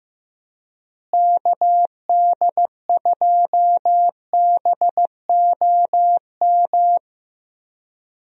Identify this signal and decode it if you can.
Morse code, heard as a keyed tone, about 15 wpm, 700 Hz. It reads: KD2BOM
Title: USA Call Signs - 6 Characters - Repeated 3x - 15wpm